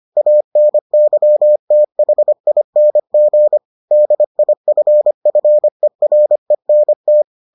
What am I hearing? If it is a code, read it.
Morse code, heard as a keyed tone, about 25 wpm, 600 Hz. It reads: ANYTHING DIFFERENT